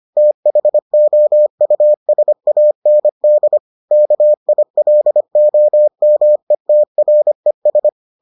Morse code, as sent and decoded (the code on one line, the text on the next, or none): - .... --- ..- ... .- -. -.. / -.- .. .-.. --- -- . - .-. ...
THOUSAND KILOMETRES